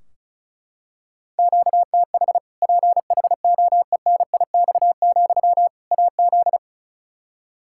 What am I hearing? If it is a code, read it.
Morse code, heard as a keyed tone, about 36 wpm, 700 Hz. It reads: QTH PHOENIX, AZ
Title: Practice QSO - Exchange 1 - Beginning - 35wpm